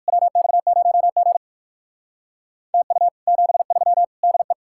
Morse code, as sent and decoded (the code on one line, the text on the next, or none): .-- -..- ----- --. / - ..- / --... ...-- / -.. .
WX0G TU 73 DE